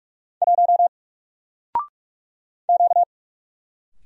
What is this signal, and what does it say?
Morse code, heard as a keyed tone, about 44 wpm, 700 Hz. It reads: 1 Q
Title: Single Letter-Number 45wpm